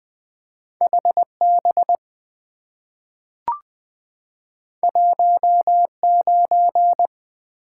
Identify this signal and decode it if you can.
Morse code, heard as a keyed tone, about 20 wpm, 700 Hz. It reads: HB 19
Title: Two Letter-Number Combinations - 20wpm